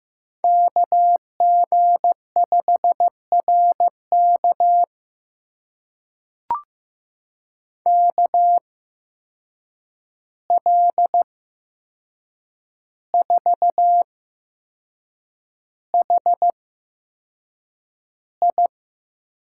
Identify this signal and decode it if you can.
Morse code, heard as a keyed tone, about 16 wpm, spaced as for 6 wpm, 700 Hz. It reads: KG5RK KL4HI
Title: USA Call Signs - 5 Characters - 8x Character Spacing - 15wpm